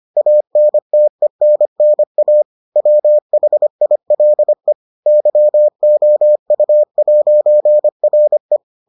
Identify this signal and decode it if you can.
Morse code, heard as a keyed tone, about 25 wpm, 600 Hz. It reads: ANTENNA WHILE YOU'RE